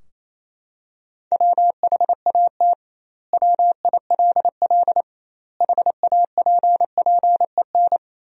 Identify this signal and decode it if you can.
Morse code, heard as a keyed tone, about 28 wpm, 700 Hz. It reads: WHAT WILL HAPPEN
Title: 2x Word-Spacing - Sentences from Top 400 Words - 28wpm